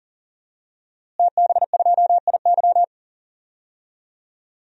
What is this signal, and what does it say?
Morse code, heard as a keyed tone, about 39 wpm, 700 Hz. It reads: TB2IY
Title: DXCC-100 Call Signs - 5 Characters - Repeated 2x - 40wpm